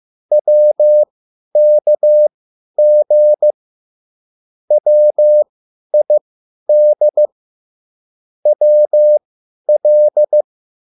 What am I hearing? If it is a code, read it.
Morse code, heard as a keyed tone, about 15 wpm, spaced as for 10 wpm, 600 Hz. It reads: WKG WID WL
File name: QSL / CW lesson